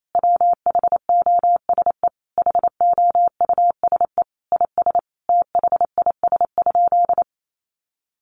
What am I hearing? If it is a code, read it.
Morse code, heard as a keyed tone, about 28 wpm, 700 Hz. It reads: WHOSE HOUSE IS THIS?
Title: Sentences from Top 1200 Words 28wpm